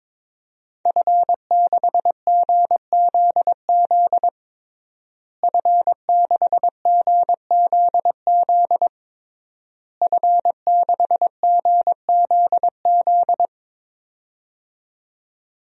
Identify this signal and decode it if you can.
Morse code, heard as a keyed tone, about 22 wpm, 700 Hz. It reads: F6GZZ F6GZZ F6GZZ
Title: DXCC-100 Call Signs - 5 Characters - Repeated 3x - 22wpm